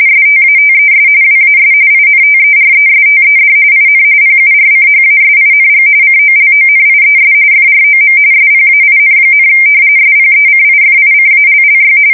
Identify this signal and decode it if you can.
RTTY, about 45.45 baud. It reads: STATES WAS INTERESTED IN PURCHASING COLONIES FROM SPAIN, SPECIFICALLY
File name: fsk170 / usnavy3